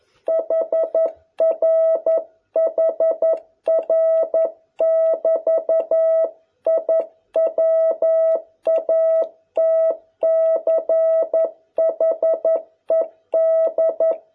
Morse code, heard as a keyed tone, about 11 wpm, 650 Hz. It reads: HR HR <BT> I WATCHED